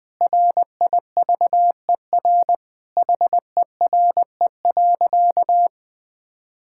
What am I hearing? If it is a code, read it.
Morse code, heard as a keyed tone, about 21 wpm, 700 Hz. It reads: RIVER HERE.
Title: Sentences from Top 1000 Words - Review Part 5 - 20wpm